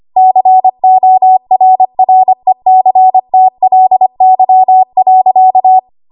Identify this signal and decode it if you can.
Morse code, heard as a keyed tone, about 25 wpm, 750 Hz. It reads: CORRECTLY.
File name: Archive / 25 WPM / 25WPM